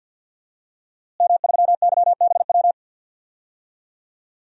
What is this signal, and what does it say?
Morse code, heard as a keyed tone, about 52 wpm, 700 Hz. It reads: M3YBW
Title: DXCC-100 Call Signs - 5 Characters - Repeated 3x - 50wpm